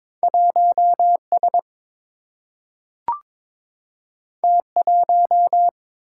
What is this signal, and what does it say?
Morse code, heard as a keyed tone, about 22 wpm, 700 Hz. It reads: 1S T1
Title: Single-Letter Single-Number 22wpm